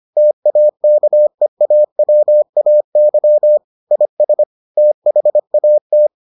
Morse code, heard as a keyed tone, about 25 wpm, 600 Hz. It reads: TAKEAWAY IS THAT